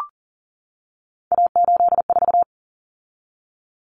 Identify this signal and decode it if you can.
Morse code, heard as a keyed tone, about 40 wpm, 700 Hz. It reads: A84
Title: Common Cut Numbers - 3-Digit 40wpm